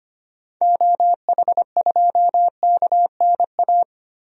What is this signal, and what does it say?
Morse code, heard as a keyed tone, about 25 wpm, 700 Hz. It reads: OH2KNA
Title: DXCC-100 Call Signs - 6 Characters - 25wpm